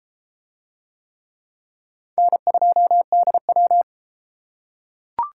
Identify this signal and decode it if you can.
Morse code, heard as a keyed tone, about 31 wpm, 700 Hz. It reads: N2DW